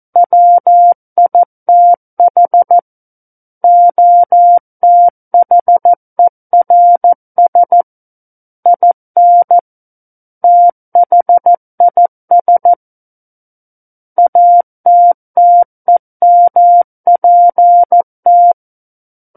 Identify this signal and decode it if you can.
Morse code, heard as a keyed tone, about 14 wpm, 700 Hz. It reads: WITH OTHERS IN THIS ATTEMPT
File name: Book One / 05-Chapter 5 - 14 WPM